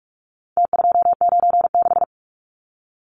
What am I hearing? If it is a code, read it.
Morse code, heard as a keyed tone, about 44 wpm, 700 Hz. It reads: T296
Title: Numbers - Contest Style - 4-Digit 45wpm